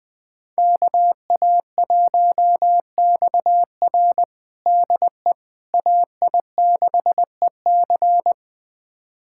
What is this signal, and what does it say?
Morse code, heard as a keyed tone, about 20 wpm, 700 Hz. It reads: KA1XR DE AI6EC
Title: USA Callsign DE Callsign - 5 Characters - 20wpm